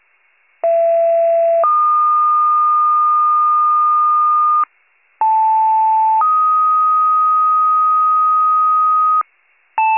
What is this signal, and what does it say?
Two-tone page, heard as a one-second tone, then a three-second tone, about 670 then 1220 Hz, about 880 then 1250 Hz. It is heard from 0.6 to 9.2 s.